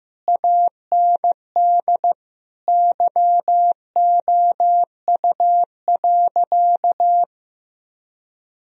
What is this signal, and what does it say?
Morse code, heard as a keyed tone, about 15 wpm, 700 Hz. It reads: AND YOU.
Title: Sentences from Top 1000 Words - Review Part 1 - 15wpm